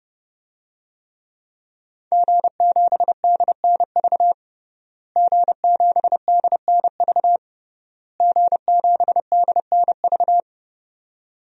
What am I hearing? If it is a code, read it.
Morse code, heard as a keyed tone, about 30 wpm, 700 Hz. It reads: G7DNV G7DNV G7DNV
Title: DXCC-100 Call Signs - 5 Characters - Repeated 3x - 30wpm